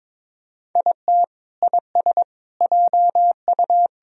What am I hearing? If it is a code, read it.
Morse code, heard as a keyed tone, about 22 wpm, 700 Hz. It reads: IT IS JU
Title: Sentences from Top 1000 Words - Review Part 4 - 22wpm